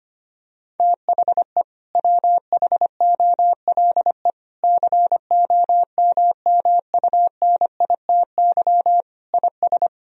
Morse code, heard as a keyed tone, about 25 wpm, 700 Hz. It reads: THE WHOLE COMMUNITY IS
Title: Sentences from Top 1000 Words - Review Part 4 - 25wpm